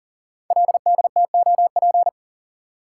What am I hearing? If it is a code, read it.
Morse code, heard as a keyed tone, about 38 wpm, 700 Hz. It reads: LDTOP
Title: Code Groups - 5 Letters - 40wpm